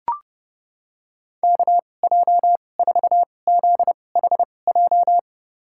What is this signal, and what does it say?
Morse code, heard as a keyed tone, about 30 wpm, spaced as for 22 wpm, 700 Hz. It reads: KJ4ZHJ
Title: USA Call Signs - 6 Characters - 2x Character Spacing - 30wpm